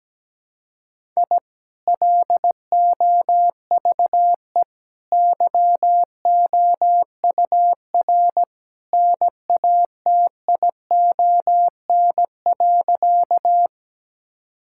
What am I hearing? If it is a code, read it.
Morse code, heard as a keyed tone, about 17 wpm, 700 Hz. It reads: I LOVE YOUR NATION.
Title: Sentences from Top 1000 Words - Review Part 6 - 17wpm